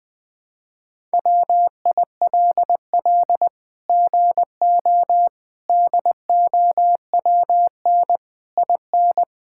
Morse code, heard as a keyed tone, about 21 wpm, 700 Hz. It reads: WILL GO DOWN IN
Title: Sets of 9 Words - 20wpm